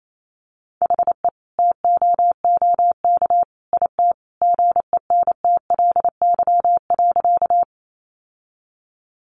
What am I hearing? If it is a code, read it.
Morse code, heard as a keyed tone, about 28 wpm, 700 Hz. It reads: HE TOOK IT GENTLY.
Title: Sentences from Top 1500 Words 28wpm